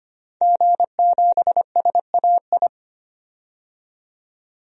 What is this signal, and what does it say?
Morse code, heard as a keyed tone, about 25 wpm, 700 Hz. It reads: G7SAI
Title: DXCC-100 Call Signs - 5 Characters - 25wpm